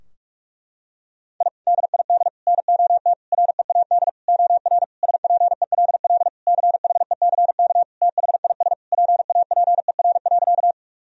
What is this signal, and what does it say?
Morse code, heard as a keyed tone, about 46 wpm, 700 Hz. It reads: I DID NOT READ OR SPELL CHECK THIS PAPER.
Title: Sentences from Top 1000 Words - Review Part 3 - 45wpm